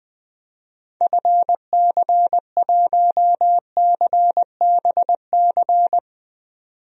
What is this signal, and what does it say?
Morse code, heard as a keyed tone, about 20 wpm, 700 Hz. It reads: FC1CBC
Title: DXCC-100 Call Signs - 6 Characters - 20wpm